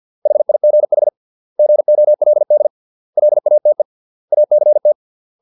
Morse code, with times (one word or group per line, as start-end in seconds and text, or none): 0.3-1.1 HIGH
1.6-2.7 GOLD
3.2-3.8 LATE
4.3-4.9 ACT